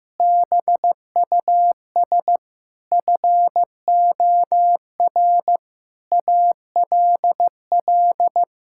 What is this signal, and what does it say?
Morse code, heard as a keyed tone, about 15 wpm, 700 Hz. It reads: BUS FOR ALL
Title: Sentences from Top 1000 Words - Review Part 6 - 15wpm